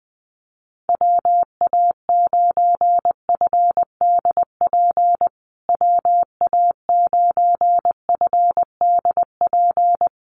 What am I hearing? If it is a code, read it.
Morse code, heard as a keyed tone, about 20 wpm, 700 Hz. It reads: WA9FDP WA9FDP
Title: USA Call Signs Repeated 2x 20wpm